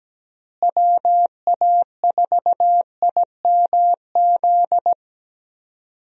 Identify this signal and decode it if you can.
Morse code, heard as a keyed tone, about 17 wpm, 700 Hz. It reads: WA4IMZ
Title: USA Call Signs - 6 Characters - Repeated 3x - 17wpm